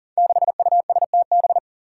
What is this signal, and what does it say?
Morse code, heard as a keyed tone, about 40 wpm, 700 Hz. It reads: 6USTB
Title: Five Challenging Letter-Number Combinations 40wpm